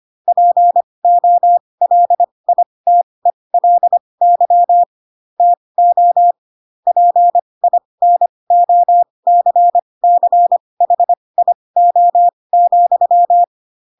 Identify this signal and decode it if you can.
Morse code, heard as a keyed tone, about 25 wpm, spaced as for 20 wpm, 700 Hz. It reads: POLITELY TO PINOCCHIO,